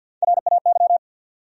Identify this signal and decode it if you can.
Morse code, heard as a keyed tone, about 51 wpm, 700 Hz. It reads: RAY
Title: USA Common Names - Max 3 Letters - 50wpm